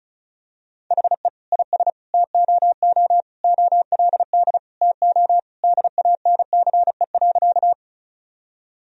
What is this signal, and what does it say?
Morse code, heard as a keyed tone, about 36 wpm, 700 Hz. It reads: HE IS TOO OLD TO DANCE.
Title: Sentences from Top 1000 Words - Review Part 4 - 35wpm